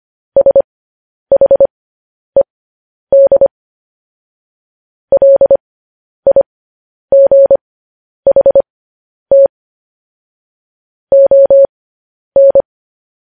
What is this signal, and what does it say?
Morse code, heard as a keyed tone, about 25 wpm, spaced as for 10 wpm, 550 Hz. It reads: SHED LIGHT ON
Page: news headlines in morse code at 10 wpm